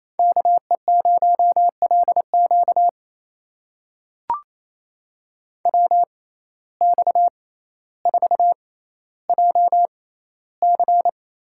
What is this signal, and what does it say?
Morse code, heard as a keyed tone, about 28 wpm, spaced as for 14 wpm, 700 Hz. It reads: KE0LQ WX4JC
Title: USA Call Signs - 6x Character Spacing - 28wpm